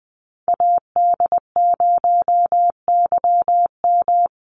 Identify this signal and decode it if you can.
Morse code, heard as a keyed tone, about 20 wpm, 700 Hz. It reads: AD0YM
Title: NA SOTA Chaser Callsigns 20wpm